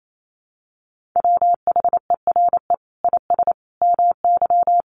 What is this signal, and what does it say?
Morse code, heard as a keyed tone, about 28 wpm, 700 Hz. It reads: WHERE IS MY
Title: Sentences from Top 1400 Words 28wpm